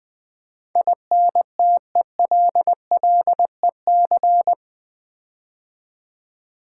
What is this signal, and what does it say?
Morse code, heard as a keyed tone, about 20 wpm, 700 Hz. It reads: INTELLEC
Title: Build A Word - 4 Letters At A Time - 12-Letter Words - 20wpm